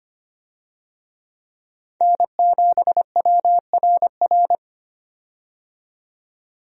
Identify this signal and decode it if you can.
Morse code, heard as a keyed tone, about 25 wpm, 700 Hz. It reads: N7WRR
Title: USA Call Signs - 5 Characters - 25wpm